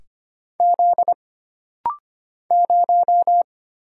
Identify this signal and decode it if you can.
Morse code, heard as a keyed tone, about 25 wpm, 700 Hz. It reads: Z 0
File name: Single Character (Letter:Number:Punctuation:Prosign) - Rapid-Fire - 25wpm